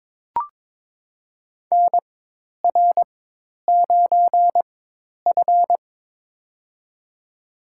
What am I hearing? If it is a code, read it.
Morse code, heard as a keyed tone, about 22 wpm, spaced as for 10 wpm, 700 Hz. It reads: NR9F
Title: USA Call Signs - 4 Characters - 4x Character Spacing - 22wpm